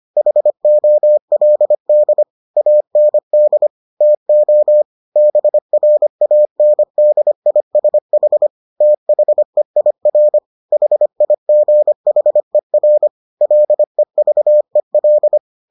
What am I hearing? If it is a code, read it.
Morse code, heard as a keyed tone, about 25 wpm, 600 Hz. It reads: HOLD AND TO BRANDISH THEIR HIGHER LEVEL